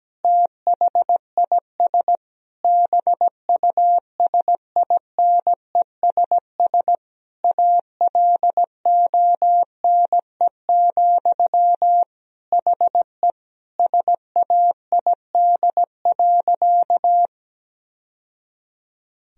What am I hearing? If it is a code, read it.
Morse code, heard as a keyed tone, about 17 wpm, 700 Hz. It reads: THIS BUSINESS ALONE, HE SAID.
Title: Sentences from Top 1000 Words - Review Part 2 - 17wpm